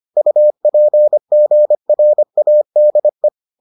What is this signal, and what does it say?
Morse code, heard as a keyed tone, about 25 wpm, 600 Hz. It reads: UPGRADE